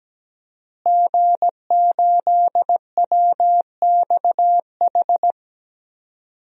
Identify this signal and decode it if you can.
Morse code, heard as a keyed tone, about 17 wpm, 700 Hz. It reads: G8WXH
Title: DXCC-100 Call Signs - 5 Characters - Repeated 2x - 17wpm